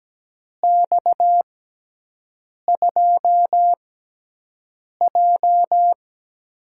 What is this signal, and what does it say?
Morse code, heard as a keyed tone, about 17 wpm, spaced as for 6 wpm, 700 Hz. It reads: X2J
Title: USA Call Signs - 6x Character Spacing - 17wpm